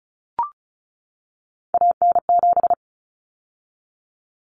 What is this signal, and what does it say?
Morse code, heard as a keyed tone, about 34 wpm, 700 Hz. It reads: AN7
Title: Common Cut Numbers - 3-Digit 35wpm